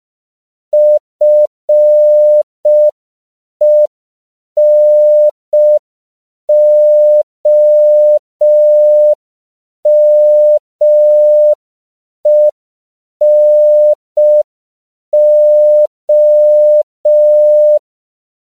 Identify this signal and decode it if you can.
Morse code, heard as a keyed tone, about 5 wpm, 600 Hz. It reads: FENOMENO
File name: CW / 5PPM